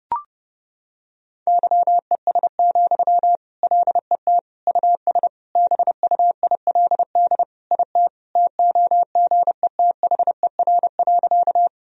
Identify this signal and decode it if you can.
Morse code, heard as a keyed tone, about 30 wpm, 700 Hz. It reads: YES, LET US BUILD IT TOGETHER.
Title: Sentences from Top 1000 Words - Review Part 1 - 30wpm